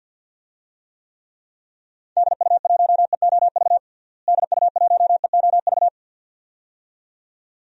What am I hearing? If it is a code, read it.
Morse code, heard as a keyed tone, about 45 wpm, 700 Hz. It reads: DU1EOV DU1EOV
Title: DXCC-100 Call Signs - 6 Characters - Repeated 2x - 50wpm